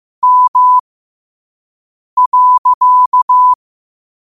Morse code, heard as a keyed tone, about 15 wpm, spaced as for 5 wpm, 1000 Hz. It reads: M.
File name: qotdfiles / 5wpm